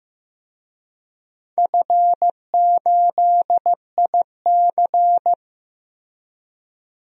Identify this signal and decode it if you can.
Morse code, heard as a keyed tone, about 15 wpm, 700 Hz. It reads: F8IC
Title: DXCC-100 Call Signs - 4 Characters - Repeated 2x - 15wpm